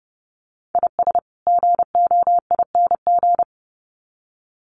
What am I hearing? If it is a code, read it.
Morse code, heard as a keyed tone, about 30 wpm, 700 Hz. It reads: IS GOING
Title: Sets of 2 Words 30wpm